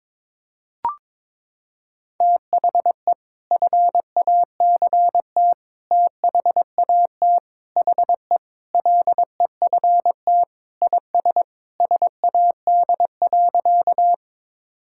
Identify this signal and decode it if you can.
Morse code, heard as a keyed tone, about 22 wpm, 700 Hz. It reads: THE FACT THAT HE LEFT IS SAD.
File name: practice / 22wpm / Sentences with 2-Word Phrases - 22wpm